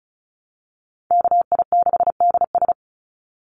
Morse code, heard as a keyed tone, about 36 wpm, 700 Hz. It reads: KI6DS
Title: NA SOTA Chaser Callsigns 35wpm